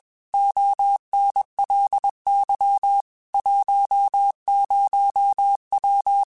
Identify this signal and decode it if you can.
Morse code, heard as a keyed tone, about 21 wpm, 800 Hz. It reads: ONLY 10W